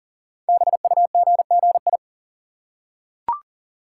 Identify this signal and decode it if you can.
Morse code, heard as a keyed tone, about 39 wpm, 690 Hz. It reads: BUGGI
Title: Code Groups - 5 Letter:Numbers - 40wpm